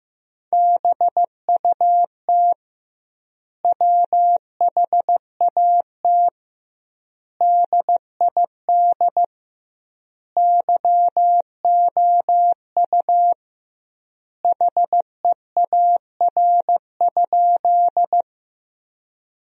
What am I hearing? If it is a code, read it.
Morse code, heard as a keyed tone, about 15 wpm, 700 Hz. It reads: BUT WHAT DID YOU HEAR?
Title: 2x Word-Spacing - Sentences from Top 300 Words - 15wpm